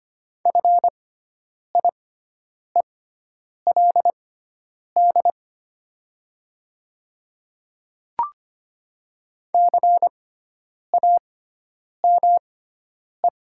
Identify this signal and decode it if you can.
Morse code, heard as a keyed tone, about 25 wpm, spaced as for 9 wpm, 700 Hz. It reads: FIELD CAME
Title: 6x Character Spacing - Top 500 Words - 25wpm